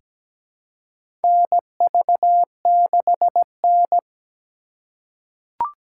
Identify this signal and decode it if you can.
Morse code, heard as a keyed tone, about 17 wpm, 700 Hz. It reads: NV6N
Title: USA Call Signs - 4 Characters - 3x Character Spacing - 17wpm